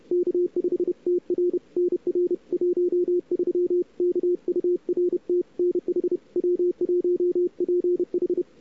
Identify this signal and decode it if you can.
Morse code, heard as a keyed tone, about 32 wpm, 410 Hz. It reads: K5TR NR 13 KURT NH W1PH
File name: cw-sprint-k5tr-op-n5rz